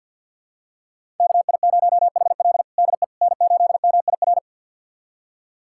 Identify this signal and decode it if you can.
Morse code, heard as a keyed tone, about 52 wpm, 700 Hz. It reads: KI0HL DE N8MIR
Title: USA Callsign DE Callsign - 5 Characters - 50wpm